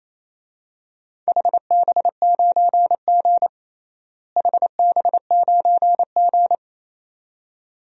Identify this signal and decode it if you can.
Morse code, heard as a keyed tone, about 27 wpm, 700 Hz. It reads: HB9G HB9G
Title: DXCC-100 Call Signs - 4 Characters - Repeated 2x - 28wpm